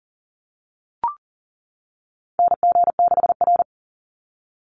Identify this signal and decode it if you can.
Morse code, heard as a keyed tone, about 40 wpm, 700 Hz. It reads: NG6R
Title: NA SOTA Chaser Callsigns 40wpm